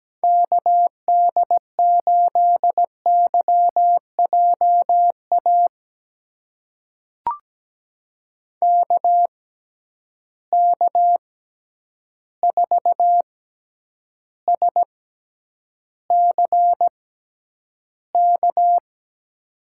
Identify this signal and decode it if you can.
Morse code, heard as a keyed tone, about 17 wpm, spaced as for 9 wpm, 700 Hz. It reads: KD8YJA KK4SCK
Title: USA Call Signs - 6 Characters - 6x Character Spacing - 17wpm